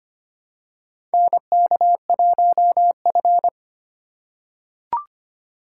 Morse code, heard as a keyed tone, about 25 wpm, 700 Hz. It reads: NK1F
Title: USA Call Signs - 3x Character Spacing - 25wpm